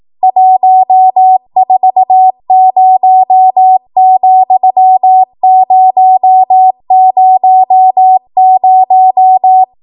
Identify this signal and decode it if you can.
Morse code, heard as a keyed tone, about 18 wpm, 750 Hz. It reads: 140,000
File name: Morse / 18WPM